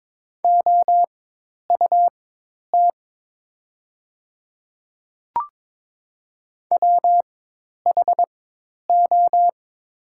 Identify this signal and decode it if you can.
Morse code, heard as a keyed tone, about 22 wpm, spaced as for 10 wpm, 700 Hz. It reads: OUT WHO
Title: 4x Character Spacing - Top 200 Words - 22wpm